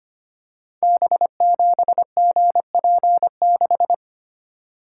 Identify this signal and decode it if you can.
Morse code, heard as a keyed tone, about 25 wpm, 700 Hz. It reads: B7GP6
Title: Code Groups - 5 Letter:Numbers - 25wpm